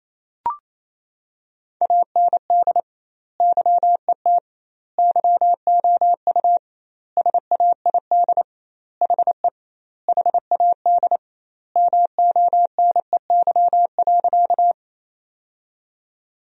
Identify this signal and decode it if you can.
Morse code, heard as a keyed tone, about 28 wpm, 700 Hz. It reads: AND YET YOU SAID HE HAD MONEY.
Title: 2x Word-Spacing - Sentences from Top 500 Words - 28wpm